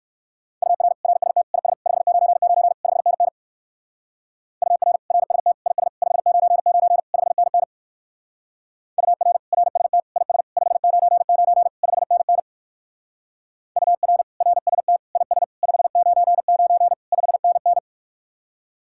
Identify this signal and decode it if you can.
Morse code, heard as a keyed tone, about 54 wpm, 700 Hz. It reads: UR RST IS 599 5NN UR RST IS 599 5NN UR RST IS 599 5NN UR RST IS 599 5NN
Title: Speed-Racing Common QSO Phrases - 45wpm